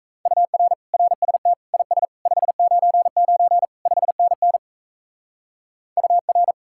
Morse code, heard as a keyed tone, about 37 wpm, 700 Hz. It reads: UR RST IS 599 5NN UR